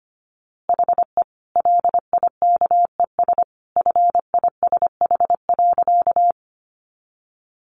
Morse code, heard as a keyed tone, about 25 wpm, 700 Hz. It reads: HE LIKES FISH.
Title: Easy Phrases 25wpm